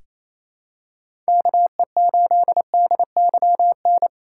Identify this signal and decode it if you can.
Morse code, heard as a keyed tone, about 28 wpm, 700 Hz. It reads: KE8DYN
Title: USA Call Signs - 4x Character Spacing - 28wpm